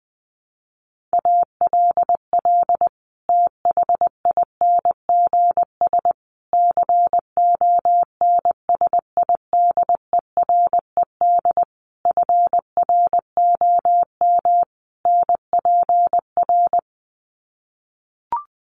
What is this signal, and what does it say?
Morse code, heard as a keyed tone, about 20 wpm, 700 Hz. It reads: ALL THINGS CONSIDERED FROM NPR
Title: Sets of 5 Words Encore 20wpm